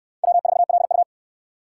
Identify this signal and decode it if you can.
Morse code, heard as a keyed tone, about 69 wpm, 700 Hz. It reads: W4LU